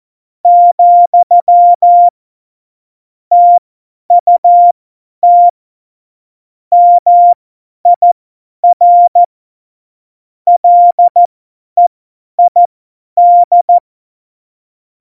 Morse code, heard as a keyed tone, about 14 wpm, spaced as for 10 wpm, 700 Hz. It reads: , TUT MIR LEID